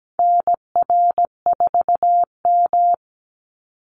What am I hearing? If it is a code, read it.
Morse code, heard as a keyed tone, about 17 wpm, 700 Hz. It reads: NR4M
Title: Call Signs Repeated 2x - 17wpm